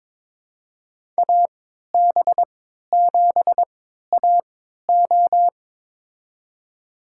Morse code, heard as a keyed tone, about 22 wpm, spaced as for 13 wpm, 700 Hz. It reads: AB7AO
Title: USA Call Signs - 3x Character Spacing - 22wpm